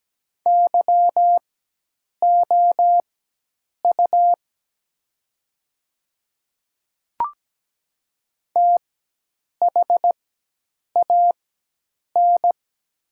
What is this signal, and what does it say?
Morse code, heard as a keyed tone, about 17 wpm, spaced as for 8 wpm, 700 Hz. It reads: YOU THAN
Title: 4x Character Spacing - Top 100 Words - 17wpm